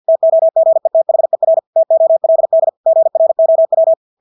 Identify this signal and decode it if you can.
Morse code, heard as a keyed tone, about 50 wpm, 650 Hz. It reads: TOGETHER TOLD GROW
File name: Engelske ord 50 wpm